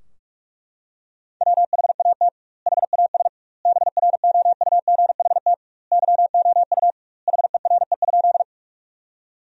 Morse code, heard as a keyed tone, about 42 wpm, 700 Hz. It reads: WHAT HAS BROUGHT YOU HERE?
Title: 2x Word-Spacing - Sentences from Top 500 Words - 45wpm